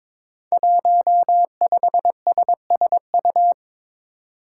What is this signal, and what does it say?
Morse code, heard as a keyed tone, about 22 wpm, 700 Hz. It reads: 15SSU
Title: Five Challenging Letter-Number Combinations 22wpm